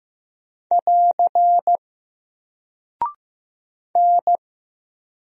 Morse code, heard as a keyed tone, about 15 wpm, 700 Hz. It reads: <AR> N
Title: Single Character (Letter:Number:Punctuation:Prosign) Alt 15wpm